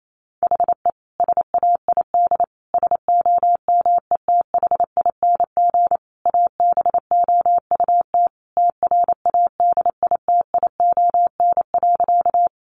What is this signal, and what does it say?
Morse code, heard as a keyed tone, about 28 wpm, 700 Hz. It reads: HE SAID SOMETHING ABOUT TRADITION.
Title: Sentences from Top 1500 Words 28wpm